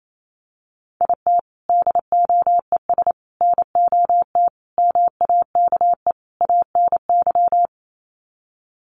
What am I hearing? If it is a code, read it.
Morse code, heard as a keyed tone, about 28 wpm, 700 Hz. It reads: IT DOES NOT MAKE ANY
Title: Sets of 5 Words Encore2 28wpm